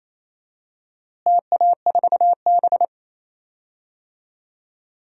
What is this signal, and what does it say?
Morse code, heard as a keyed tone, about 27 wpm, 700 Hz. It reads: TA4B
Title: DXCC-100 Call Signs - 4 Characters - Repeated 2x - 28wpm